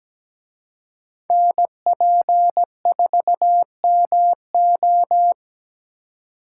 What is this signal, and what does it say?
Morse code, heard as a keyed tone, about 20 wpm, 700 Hz. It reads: NP4MO
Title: Speed Racing - USA 2x2 Call Signs 17wpm